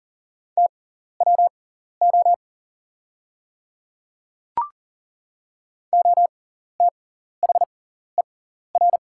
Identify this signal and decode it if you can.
Morse code, heard as a keyed tone, about 40 wpm, spaced as for 14 wpm, 690 Hz. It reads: TWO OTHER
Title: 6x Character Spacing - Top 100 Words - 40wpm